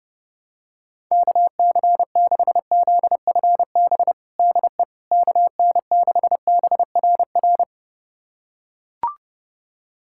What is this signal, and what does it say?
Morse code, heard as a keyed tone, about 30 wpm, 700 Hz. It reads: KC6ZFB DE KN6BRR
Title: USA Callsign DE Callsign 30wpm